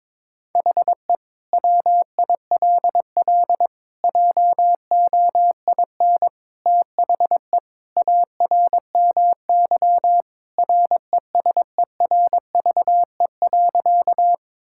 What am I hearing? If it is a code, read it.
Morse code, heard as a keyed tone, about 22 wpm, 700 Hz. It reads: HE WILL JOIN THE ARMY RESERVE.
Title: Sentences from Top 1000 Words - Review Part 6 - 22wpm